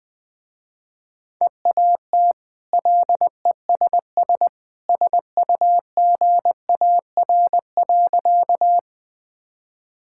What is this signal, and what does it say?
Morse code, heard as a keyed tone, about 20 wpm, 700 Hz. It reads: EAT LESS SUGAR.
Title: Easy Sentences with Hints - 20wpm